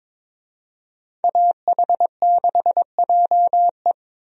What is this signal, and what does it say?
Morse code, heard as a keyed tone, about 21 wpm, 700 Hz. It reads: AH6JE
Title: USA Call Signs - 5 Characters - 22wpm